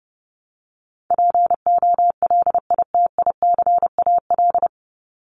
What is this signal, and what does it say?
Morse code, heard as a keyed tone, about 30 wpm, 700 Hz. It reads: POLITICAL
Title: Top 501-750 Words 30wpm